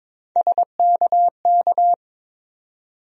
Morse code, heard as a keyed tone, about 22 wpm, 700 Hz. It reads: SKK
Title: USA Call Sign Postfixes - 3 Letters 22wpm